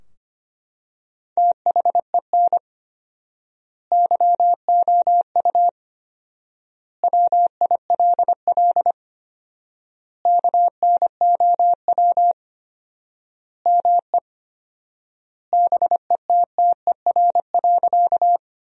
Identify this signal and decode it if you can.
Morse code, heard as a keyed tone, about 25 wpm, 700 Hz. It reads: THEN YOU WILL KNOW ME BETTER.
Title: 4x Word-Spacing - Sentences from Top 400 Words - 25wpm